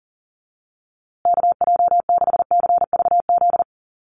Morse code, heard as a keyed tone, about 40 wpm, 700 Hz. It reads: KJ6CVZ
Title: USA Call Signs Repeated 2x 40wpm